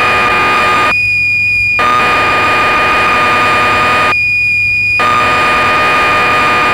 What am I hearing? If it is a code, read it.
Cassette data (Kansas City type).